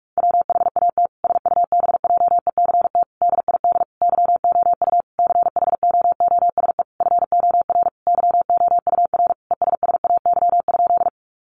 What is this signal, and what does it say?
Morse code, heard as a keyed tone, about 44 wpm, 700 Hz. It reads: WHAT SUBJECT DID YOU CHOOSE FOR YOUR ESSAY?
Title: Sentences from Top 1500 Words 45wpm